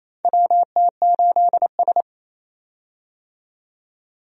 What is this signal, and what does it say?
Morse code, heard as a keyed tone, about 28 wpm, 700 Hz. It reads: WT8S
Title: USA Call Signs - 4 Characters - 28wpm